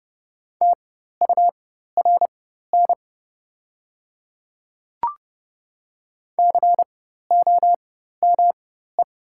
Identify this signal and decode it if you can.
Morse code, heard as a keyed tone, about 30 wpm, spaced as for 14 wpm, 700 Hz. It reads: TURN COME
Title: 4x Character Spacing - Top 200 Words - 30wpm